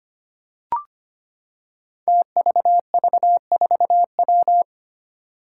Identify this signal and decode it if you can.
Morse code, heard as a keyed tone, about 25 wpm, 700 Hz. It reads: TVV4W
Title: Five Challenging Letter-Number Combinations 25wpm